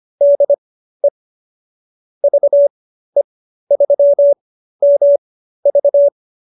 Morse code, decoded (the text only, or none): DE VE3MV